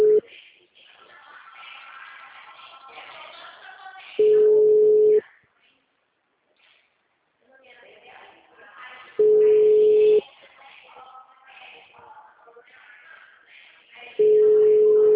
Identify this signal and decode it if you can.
Ringback tone (Europe, 425 Hz), with speech.